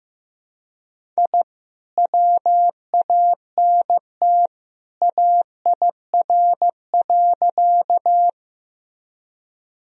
Morse code, heard as a keyed tone, about 15 wpm, 700 Hz. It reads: I WANT AIR.
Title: Sentences from Top 1000 Words - Review Part 1 - 15wpm